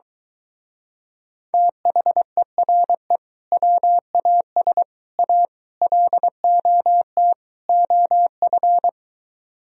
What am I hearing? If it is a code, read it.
Morse code, heard as a keyed tone, about 23 wpm, 700 Hz. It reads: THERE WAS A LOT OF